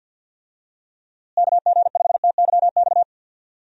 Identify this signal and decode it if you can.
Morse code, heard as a keyed tone, about 52 wpm, 700 Hz. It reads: KG5TYX
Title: USA Call Signs - 6 Characters - Repeated 3x - 50wpm